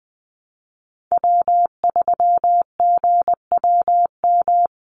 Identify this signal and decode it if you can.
Morse code, heard as a keyed tone, about 20 wpm, 700 Hz. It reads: W3GWM
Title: Call Signs Repeated 2x - 20wpm